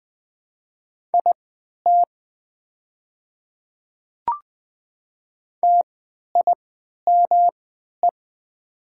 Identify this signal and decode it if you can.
Morse code, heard as a keyed tone, about 20 wpm, spaced as for 11 wpm, 690 Hz. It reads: IT TIME
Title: 3x Character Spacing - Top 100 Words - 20wpm